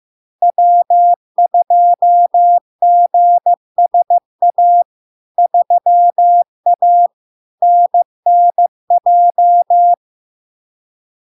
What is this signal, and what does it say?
Morse code, heard as a keyed tone, about 15 wpm, 700 Hz. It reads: W2GSA 3A NNJ
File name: LARK-FD-2022-Log-in-CW-15wpm-700-Hz